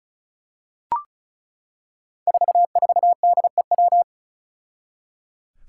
Morse code, heard as a keyed tone, about 34 wpm, 700 Hz. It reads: 44DEW
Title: Five Challenging Letter-Number Combinations 35wpm